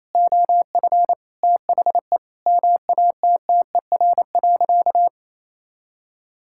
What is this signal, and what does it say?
Morse code, heard as a keyed tone, about 28 wpm, 700 Hz. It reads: OF THE MATTER.
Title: Sentences from Top 1000 Words - Review Part 6 - 28wpm